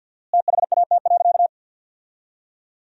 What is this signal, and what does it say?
Morse code, heard as a keyed tone, about 50 wpm, 700 Hz. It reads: THAT.
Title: 4x Word-Spacing - Sentences from Top 500 Words - 50wpm